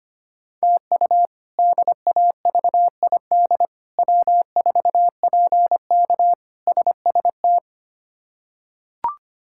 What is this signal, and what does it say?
Morse code, heard as a keyed tone, about 25 wpm, 700 Hz. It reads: TU DAVID W4PK SST
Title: K1USN SST QSO Practice - 25wpm